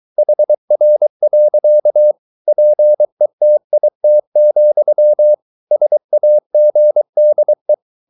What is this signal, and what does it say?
Morse code, heard as a keyed tone, about 23 wpm, 600 Hz. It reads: HR. PETIT, SAGDE
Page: Ferie 23 wpm